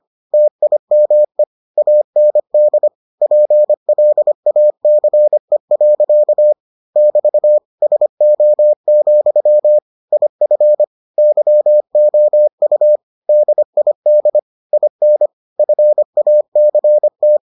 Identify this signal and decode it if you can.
Morse code, heard as a keyed tone, about 25 wpm, 600 Hz. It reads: TIME AND PLACE. <BT> SO, IF YOU DID IN FACT